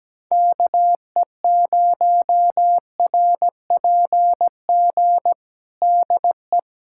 Morse code, heard as a keyed tone, about 17 wpm, 700 Hz. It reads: KE0RPG DE
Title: USA Callsign DE Callsign - 6 Characters - 17wpm